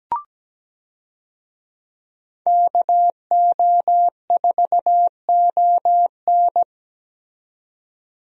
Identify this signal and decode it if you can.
Morse code, heard as a keyed tone, about 17 wpm, 700 Hz. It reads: KO4ON
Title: USA Call Signs - 5 Characters - 17wpm